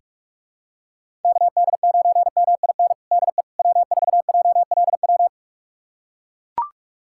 Morse code, heard as a keyed tone, about 42 wpm, 700 Hz. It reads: KD9MIN DE W4JLW
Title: USA Callsign DE Callsign 45wpm